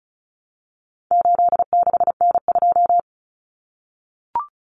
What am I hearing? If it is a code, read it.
Morse code, heard as a keyed tone, about 34 wpm, 700 Hz. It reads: 86N2
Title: Common Cut Numbers - 4-Digit 35wpm